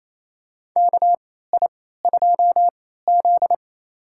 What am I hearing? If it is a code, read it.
Morse code, heard as a keyed tone, about 28 wpm, spaced as for 16 wpm, 700 Hz. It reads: KI2Z
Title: USA Call Signs - 4 Characters - 3x Character Spacing - 28wpm